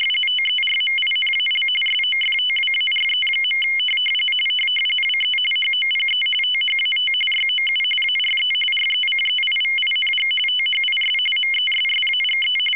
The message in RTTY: TERMINAL SOFTWARE. WE HOPE THAT YOU ARE RECEIVING THIS MESSAGE